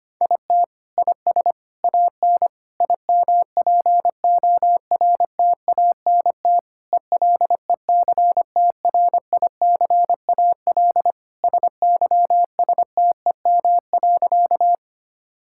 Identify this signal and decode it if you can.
Morse code, heard as a keyed tone, about 25 wpm, 700 Hz. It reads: IT IS AN IMPORTANT ELECTRICAL SYSTEM.
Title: Sentences from Top 1000 Words - Review Part 6 - 25wpm